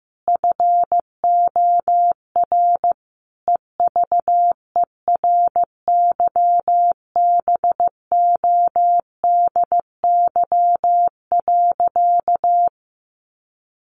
Morse code, heard as a keyed tone, about 15 wpm, 700 Hz. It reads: FOR EVERYBODY.
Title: Sentences from Top 1400 Words 15wpm